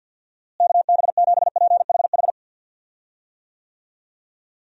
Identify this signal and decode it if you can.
Morse code, heard as a keyed tone, about 51 wpm, 700 Hz. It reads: KB7PHH
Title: USA Call Signs - 6 Characters - Repeated 3x - 50wpm